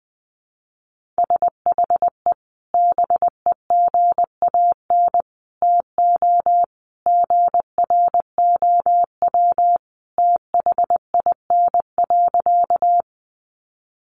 Morse code, heard as a keyed tone, about 19 wpm, 700 Hz. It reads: SHE BEGAN TO GROW THIN.
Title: Sentences from Top 1200 Words 20wpm